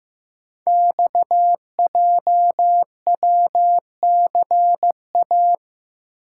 Morse code, heard as a keyed tone, about 15 wpm, 700 Hz. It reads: XJWCA
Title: Code Groups - 5 Letter:Numbers - 15wpm